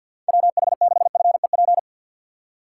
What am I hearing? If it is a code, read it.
Morse code, heard as a keyed tone, about 50 wpm, 700 Hz. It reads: WH6FEP
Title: USA Call Signs - 6 Characters - 3x Character Spacing - 50wpm